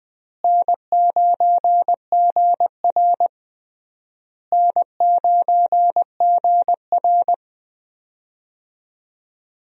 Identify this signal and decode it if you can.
Morse code, heard as a keyed tone, about 21 wpm, 700 Hz. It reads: N9GR N9GR
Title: USA Call Signs - 4 Characters - Repeated 3x - 20wpm